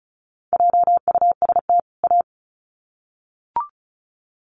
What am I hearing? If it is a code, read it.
Morse code, heard as a keyed tone, about 34 wpm, 700 Hz. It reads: JUST A
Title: Sets of 2 Words 35wpm